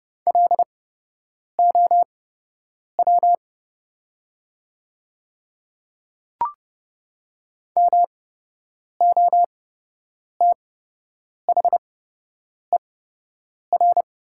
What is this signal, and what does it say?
Morse code, heard as a keyed tone, about 30 wpm, spaced as for 8 wpm, 690 Hz. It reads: LOW MOTHER
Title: 8x Character Spacing - Top 300 Words - 30wpm